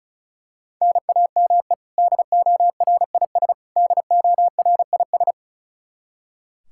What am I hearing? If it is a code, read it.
Morse code, heard as a keyed tone, about 36 wpm, 700 Hz. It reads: NAME DORIS DORIS
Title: Practice QSO - Exchange 1 - Beginning - 35wpm